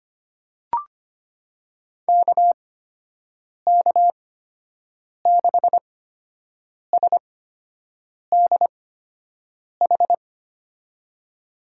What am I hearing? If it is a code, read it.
Morse code, heard as a keyed tone, about 25 wpm, spaced as for 7 wpm, 700 Hz. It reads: KK6SDH
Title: USA Call Signs - 6 Characters - 8x Character Spacing - 25wpm